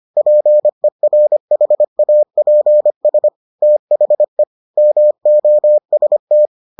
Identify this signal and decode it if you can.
Morse code, heard as a keyed tone, about 25 wpm, 600 Hz. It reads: PERHAPS THE MOST